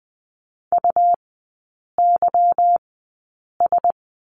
Morse code, heard as a keyed tone, about 20 wpm, spaced as for 8 wpm, 700 Hz. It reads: UYS
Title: 172 Lesson 28e - Number 9 - Three Characters 20wpm